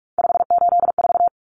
Morse code, heard as a keyed tone, about 44 wpm, 700 Hz. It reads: 584
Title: Common Cut Numbers - 3-Digit 45wpm